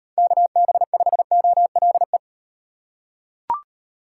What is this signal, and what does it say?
Morse code, heard as a keyed tone, about 35 wpm, 700 Hz. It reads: KB5OLE